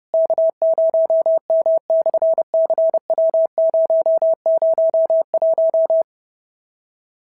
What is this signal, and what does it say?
Morse code, heard as a keyed tone, about 30 wpm, 650 Hz. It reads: K0M/CW001
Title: SOTA QSO Practice 30wpm